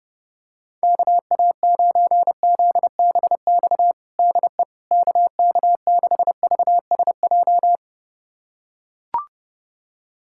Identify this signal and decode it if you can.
Morse code, heard as a keyed tone, about 32 wpm, 700 Hz. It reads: KA9ZBX DE KK6VSJ
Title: USA Callsign DE Callsign - 6 Characters - 30wpm